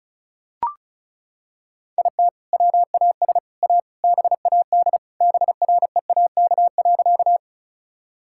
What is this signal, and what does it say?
Morse code, heard as a keyed tone, about 34 wpm, 700 Hz. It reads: IT WAS A BAD BREAK.
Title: Sentences from Top 1000 Words - Review Part 6 - 35wpm